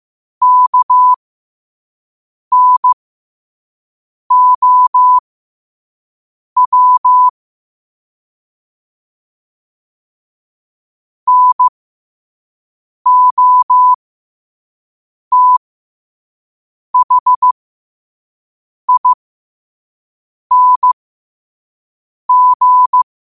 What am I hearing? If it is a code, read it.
Morse code, heard as a keyed tone, about 15 wpm, spaced as for 5 wpm, 1000 Hz. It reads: KNOW NOTHING
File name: qotdfiles / 5wpm